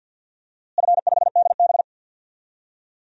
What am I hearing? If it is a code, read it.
Morse code, heard as a keyed tone, about 46 wpm, 700 Hz. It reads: F5DB
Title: DXCC-100 Call Signs - 4 Characters - Repeated 2x - 50wpm